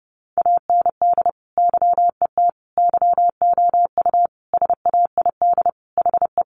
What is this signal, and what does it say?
Morse code, heard as a keyed tone, about 30 wpm, 700 Hz. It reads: AND YET YOU SAID HE
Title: Sentences from Top 500 Words 30wpm